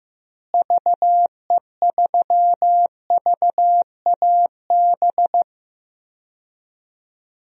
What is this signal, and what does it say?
Morse code, heard as a keyed tone, about 15 wpm, 700 Hz. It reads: VE3VAB
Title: DXCC-100 Call Signs - 6 Characters - Repeated 3x - 15wpm